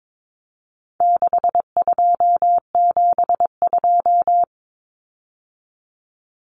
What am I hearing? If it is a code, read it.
Morse code, heard as a keyed tone, about 22 wpm, 700 Hz. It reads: 6272
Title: Common Cut Numbers - 4-Digit 22wpm